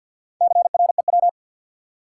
Morse code, heard as a keyed tone, about 48 wpm, 700 Hz. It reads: CREW
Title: Sentences from Top 1500 Words - New Word Drill Alt - 50wpm